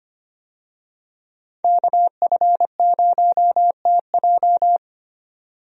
Morse code, heard as a keyed tone, about 25 wpm, 700 Hz. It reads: KF0TJ
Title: USA Call Signs - 5 Characters - 25wpm